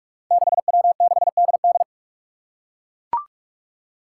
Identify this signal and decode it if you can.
Morse code, heard as a keyed tone, about 43 wpm, 700 Hz. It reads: 6W6DD
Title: Five Challenging Letter-Number Combinations 45wpm